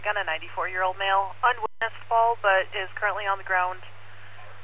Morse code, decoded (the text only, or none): none